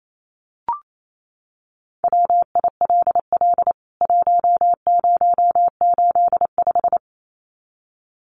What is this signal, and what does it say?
Morse code, heard as a keyed tone, about 28 wpm, 700 Hz. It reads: WILL 1085
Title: CWT Exchange 28wpm